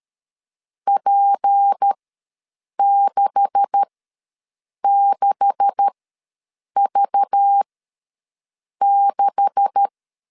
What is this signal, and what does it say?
Morse code, heard as a keyed tone, about 13 wpm, spaced as for 7 wpm, 800 Hz. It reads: P66V6
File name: LEZIONI / 9 286 ws 12wpm 3351